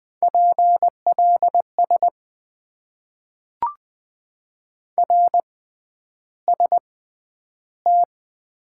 Morse code, heard as a keyed tone, about 20 wpm, spaced as for 10 wpm, 700 Hz. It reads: PLS RST
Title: Top 90 QSO Elements - 6x Character Spacing - 20wpm